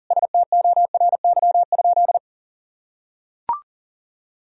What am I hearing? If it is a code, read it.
Morse code, heard as a keyed tone, about 37 wpm, 700 Hz. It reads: STORY?
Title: Sentences from Top 400 Words and Abbreviations 40wpm